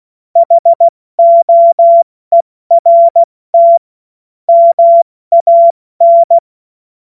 Morse code, heard as a keyed tone, about 16 wpm, 680 Hz. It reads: HOERT MAN
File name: Morsekurs / Witze / Witz 31. Mai 2022 70 F80